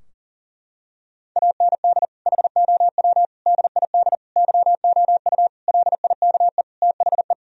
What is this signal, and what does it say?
Morse code, heard as a keyed tone, about 40 wpm, 700 Hz. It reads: AND HOW DID YOU LIKE THE
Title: Sentences from Top 1000 Words - Review Part 4 - 40wpm